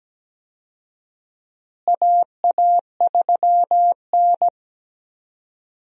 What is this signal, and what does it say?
Morse code, heard as a keyed tone, about 17 wpm, 700 Hz. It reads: AA3N
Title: USA Call Signs - 4 Characters - Repeated 2x - 17wpm